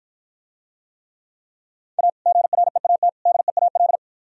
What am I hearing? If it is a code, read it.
Morse code, heard as a keyed tone, about 54 wpm, 700 Hz. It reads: A GREAT DEAL